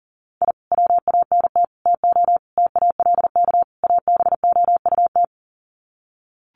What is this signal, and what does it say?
Morse code, heard as a keyed tone, about 37 wpm, 700 Hz. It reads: I WANT TO TALK ABOUT
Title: Sets of 5 Words Encore 40wpm